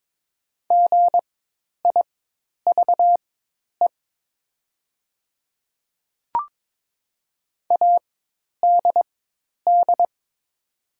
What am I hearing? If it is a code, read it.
Morse code, heard as a keyed tone, about 22 wpm, spaced as for 10 wpm, 700 Hz. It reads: GIVE ADD
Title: 4x Character Spacing - Top 300 Words - 22wpm